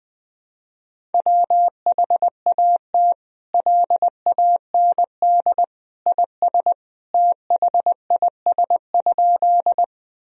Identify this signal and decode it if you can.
Morse code, heard as a keyed tone, about 20 wpm, 700 Hz. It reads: WHAT LAND IS THIS?
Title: Sentences from Top 1000 Words - Review Part 1 - 20wpm